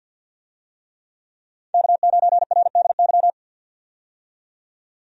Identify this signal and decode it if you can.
Morse code, heard as a keyed tone, about 52 wpm, 700 Hz. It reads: K9RDY
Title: USA Call Signs - 5 Characters - 50wpm